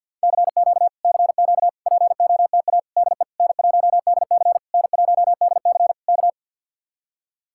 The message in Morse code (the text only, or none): CQ CQ POTA DE N1DC N1DC K